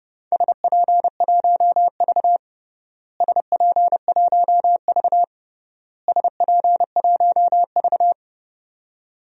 Morse code, heard as a keyed tone, about 30 wpm, 700 Hz. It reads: SP1V SP1V SP1V
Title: DXCC-100 Call Signs - 4 Characters - Repeated 3x - 30wpm